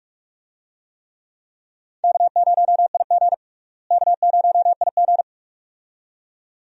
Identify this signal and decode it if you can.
Morse code, heard as a keyed tone, about 46 wpm, 700 Hz. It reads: K0IG K0IG
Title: USA Call Signs - 4 Characters - Repeated 2x - 45wpm